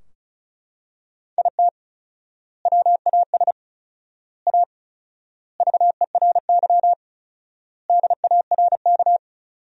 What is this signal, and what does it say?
Morse code, heard as a keyed tone, about 36 wpm, 700 Hz. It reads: IT WAS A VERY DARK
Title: 4x Word-Spacing - Sentences from Top 500 Words - 35wpm